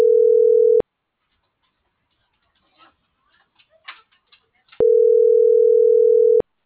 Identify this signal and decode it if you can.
Ringback tone (North America), 440 + 480 Hz.
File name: CENTRAL TERRESTRE / out cris.f. - .22009